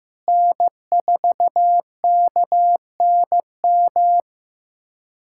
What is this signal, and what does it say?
Morse code, heard as a keyed tone, about 15 wpm, 700 Hz. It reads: N4KNM
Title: USA Callsign DE Callsign - 5 Characters - 15wpm